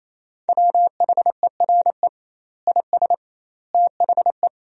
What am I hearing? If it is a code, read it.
Morse code, heard as a keyed tone, about 28 wpm, 700 Hz. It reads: WHERE IS THE
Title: 2x Word-Spacing - Sentences from Top 300 Words - 28wpm